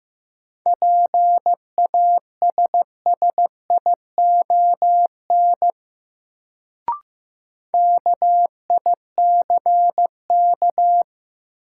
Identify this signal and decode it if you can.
Morse code, heard as a keyed tone, about 15 wpm, 700 Hz. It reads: PASSION KICK
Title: Sentences from Top 1300 Words - New Word Drill - 15wpm